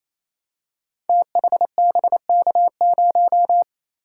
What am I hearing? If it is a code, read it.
Morse code, heard as a keyed tone, about 27 wpm, 700 Hz. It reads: THBK0
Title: Code Groups - 5 Letter:Numbers - 28wpm